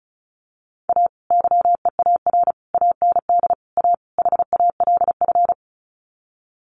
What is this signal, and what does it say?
Morse code, heard as a keyed tone, about 34 wpm, 700 Hz. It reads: A YEAR AND A HALF
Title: Sets of 5 Words 35wpm